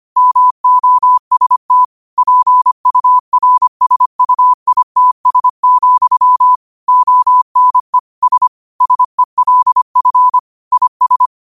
Morse code, heard as a keyed tone, about 25 wpm, 1000 Hz. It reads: MOST PURSUITS, ONE S SELF IS
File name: qotdfiles / 25wpm